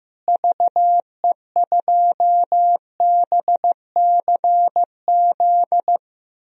Morse code, heard as a keyed tone, about 15 wpm, 700 Hz. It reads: VE2BCZ
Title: DXCC-100 Call Signs - 6 Characters - Repeated 2x - 15wpm